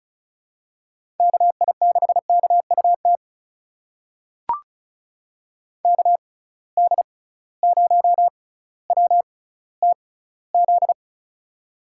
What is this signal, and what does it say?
Morse code, heard as a keyed tone, about 36 wpm, spaced as for 18 wpm, 700 Hz. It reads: KI6KUT KD0WTZ
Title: USA Call Signs - 6 Characters - 6x Character Spacing - 35wpm